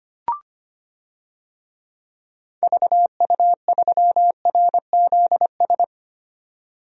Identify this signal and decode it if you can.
Morse code, heard as a keyed tone, about 25 wpm, 700 Hz. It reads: VU3RZS
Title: DXCC-100 Call Signs - 6 Characters - 25wpm